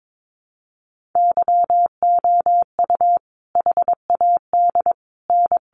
Morse code, heard as a keyed tone, about 22 wpm, 700 Hz. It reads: YOU HAD N
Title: Sentences from Top 1300 Words 22wpm